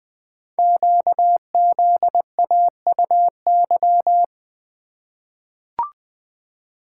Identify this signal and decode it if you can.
Morse code, heard as a keyed tone, about 20 wpm, 700 Hz. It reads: QZAUY
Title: Code Groups - 5 Letters - 20wpm